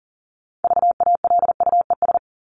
Morse code, heard as a keyed tone, about 40 wpm, 700 Hz. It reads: VALUES
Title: ES:S Suffix - Top 100 Words with Base Words 40wpm